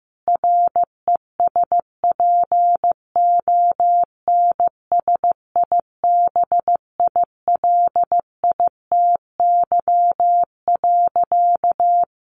Morse code, heard as a keyed tone, about 15 wpm, 700 Hz. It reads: RESPONSIBILITY.
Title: Sentences from Top 1500 Words 15wpm